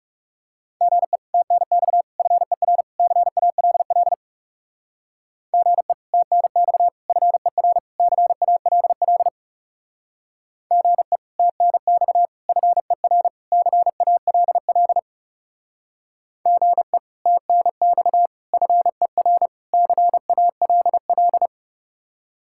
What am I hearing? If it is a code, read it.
Morse code, heard as a keyed tone, about 36 wpm, 700 Hz. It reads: GE TNX FER CALL GE TNX FER CALL GE TNX FER CALL GE TNX FER CALL
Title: Speed-Racing Common QSO Phrases - 30wpm